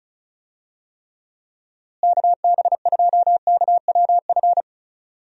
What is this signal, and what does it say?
Morse code, heard as a keyed tone, about 36 wpm, 700 Hz. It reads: KB2KWF
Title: USA Call Signs - 6 Characters - Repeated 2x - 35wpm